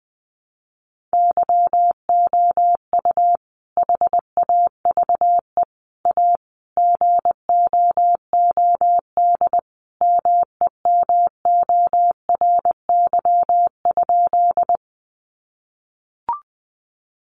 Morse code, heard as a keyed tone, about 20 wpm, 700 Hz. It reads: YOU HAVE A GOOD MEMORY?
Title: Sentences from Top 1300 Words 20wpm